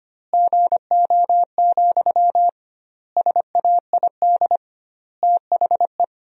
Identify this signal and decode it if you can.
Morse code, heard as a keyed tone, about 25 wpm, 700 Hz. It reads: GO, SAID THE
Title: 2x Word-Spacing - Sentences from Top 400 Words - 25wpm